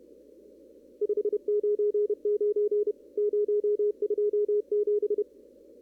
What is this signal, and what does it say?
Morse code, heard as a keyed tone, about 30 wpm, 1180 Hz. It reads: 599 027
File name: CW / kt5j wpx cw-op-k5pi